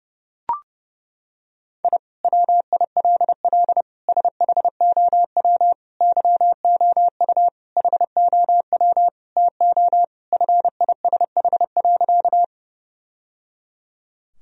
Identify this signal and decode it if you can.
Morse code, heard as a keyed tone, about 30 wpm, 700 Hz. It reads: I WILL SHOW YOU HOW TO FISH.
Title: Sentences from Top 1000 Words - Review Part 1 - 30wpm